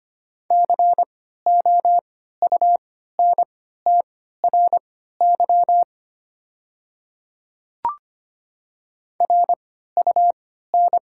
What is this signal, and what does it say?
Morse code, heard as a keyed tone, about 25 wpm, spaced as for 14 wpm, 690 Hz. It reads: COUNTRY RUN
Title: 3x Character Spacing - Top 300 Words - 25wpm